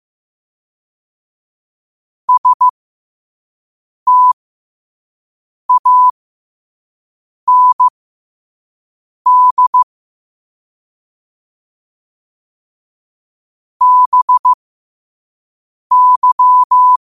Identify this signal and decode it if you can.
Morse code, heard as a keyed tone, about 15 wpm, spaced as for 5 wpm, 1000 Hz. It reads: STAND BY
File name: qotdfiles / 5wpm